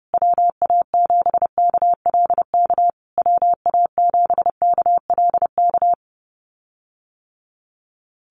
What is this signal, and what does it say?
Morse code, heard as a keyed tone, about 30 wpm, 700 Hz. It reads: WA7KLK WA7KLK
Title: USA Call Signs Repeated 3x 30wpm